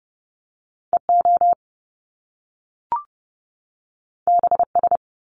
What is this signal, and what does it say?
Morse code, heard as a keyed tone, about 30 wpm, 700 Hz. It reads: EO BS
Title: Top Two Letter Combinations in Words 30wpm